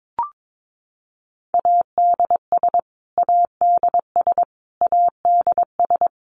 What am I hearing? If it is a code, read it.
Morse code, heard as a keyed tone, about 22 wpm, 700 Hz. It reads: ADS ADS ADS
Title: Abbreviations Repeated 3x - 22wpm